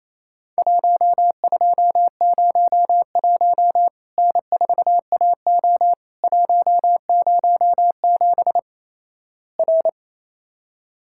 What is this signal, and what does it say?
Morse code, heard as a keyed tone, about 28 wpm, 700 Hz. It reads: 1201 N4AO 107 R
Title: Worked All Europe DX Contest - QTC Exchange - 28wpm